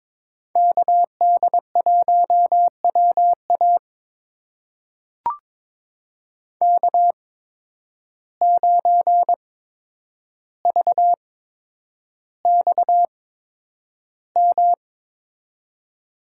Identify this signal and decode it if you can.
Morse code, heard as a keyed tone, about 22 wpm, spaced as for 9 wpm, 700 Hz. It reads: KD1WA K9VXM
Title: USA Call Signs - 5 Characters - 8x Character Spacing - 22wpm